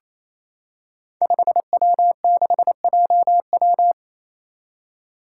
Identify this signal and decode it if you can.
Morse code, heard as a keyed tone, about 28 wpm, 700 Hz. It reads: 5W6JW
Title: Five Challenging Letter-Number Combinations 28wpm